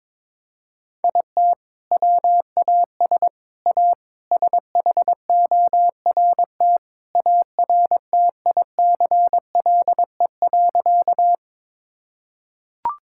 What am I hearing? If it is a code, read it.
Morse code, heard as a keyed tone, about 22 wpm, 700 Hz. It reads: IT WAS A SHORT ARTICLE.
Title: Sentences from Top 1000 Words - Review Part 5 - 22wpm